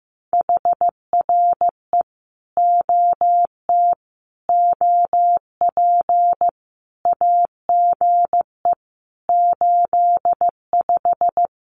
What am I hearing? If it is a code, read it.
Morse code, heard as a keyed tone, about 15 wpm, 700 Hz. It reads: HRE OT OP AGE 85
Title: Common QSO Phrases 15wpm